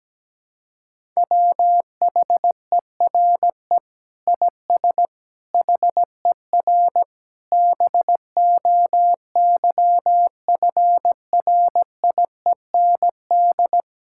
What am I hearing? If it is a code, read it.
Morse code, heard as a keyed tone, about 17 wpm, 700 Hz. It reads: WHERE IS HER BOYFRIEND
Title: Sentences from Top 1000 Words - Review Part 5 - 17wpm